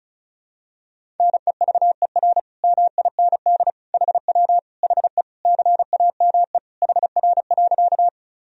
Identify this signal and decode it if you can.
Morse code, heard as a keyed tone, about 34 wpm, 700 Hz. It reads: NEVER MIND HW HE CAME HR.
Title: Sentences from Top 500 Words and Abbreviations 35wpm